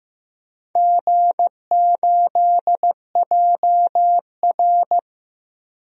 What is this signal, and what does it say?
Morse code, heard as a keyed tone, about 16 wpm, 700 Hz. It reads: G8JR
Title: DXCC-100 Call Signs - 4 Characters - Repeated 2x - 15wpm